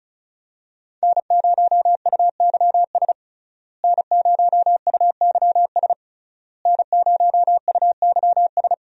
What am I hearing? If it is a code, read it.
Morse code, heard as a keyed tone, about 36 wpm, 700 Hz. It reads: N0UYS N0UYS N0UYS
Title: USA Call Signs - 5 Characters - Repeated 3x - 35wpm